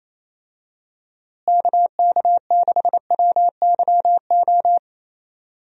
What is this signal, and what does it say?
Morse code, heard as a keyed tone, about 28 wpm, 700 Hz. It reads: KK6WYO
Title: USA Call Signs - 6 Characters - Repeated 2x - 28wpm